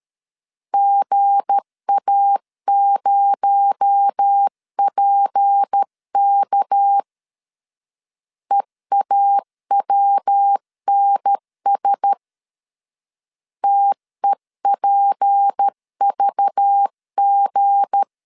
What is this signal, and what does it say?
Morse code, heard as a keyed tone, about 13 wpm, 800 Hz. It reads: GA0PK EAWNS TEPVG
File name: LEZIONI / 7 GF9Z / 7 G 12wpm 2055